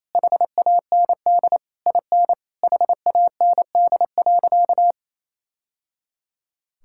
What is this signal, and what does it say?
Morse code, heard as a keyed tone, about 28 wpm, 700 Hz. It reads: HAND IN HAND.
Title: Sentences from Top 1000 Words - Review Part 5 - 28wpm